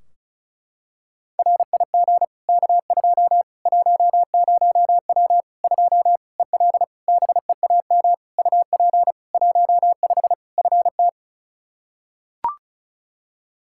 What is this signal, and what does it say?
Morse code, heard as a keyed tone, about 36 wpm, 700 Hz. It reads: RIG K2 10W 2 EL BEAM UP 15 FT
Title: Practice QSO - Exchange 2 - Info - 35wpm